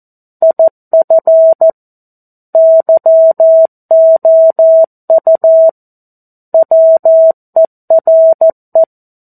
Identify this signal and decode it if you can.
Morse code, heard as a keyed tone, about 14 wpm, 650 Hz. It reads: IF YOU WERE
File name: 02-Chapter-02 14 WPM